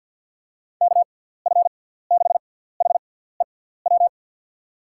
Morse code, heard as a keyed tone, about 52 wpm, 700 Hz. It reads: K F 6 H E W
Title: USA Call Signs - 6x Character Spacing - 50wpm